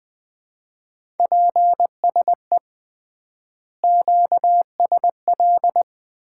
Morse code, heard as a keyed tone, about 20 wpm, 700 Hz. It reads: PSE QSL
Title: Common QSO Phrases - 3x Word-Spacing - 20wpm